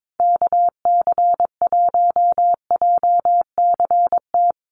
Morse code, heard as a keyed tone, about 22 wpm, 700 Hz. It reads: KC1JCT
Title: USA Call Signs 22wpm